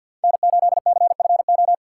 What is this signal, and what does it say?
Morse code, heard as a keyed tone, about 52 wpm, 700 Hz. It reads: N8CFO
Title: USA Call Signs - 5 Characters - 6x Character Spacing - 50wpm